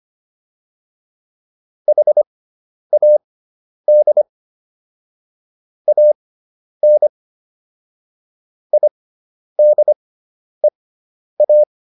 Morse code, heard as a keyed tone, about 25 wpm, spaced as for 10 wpm, 600 Hz. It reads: HAD AN IDEA